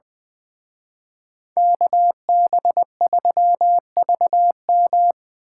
Morse code, heard as a keyed tone, about 25 wpm, 700 Hz. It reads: KB3VM
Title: Speed Racing - USA 2x2 Call Signs 20wpm